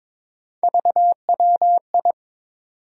Morse code, heard as a keyed tone, about 21 wpm, 700 Hz. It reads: VWI
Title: USA Call Sign Postfixes - 3 Letters 22wpm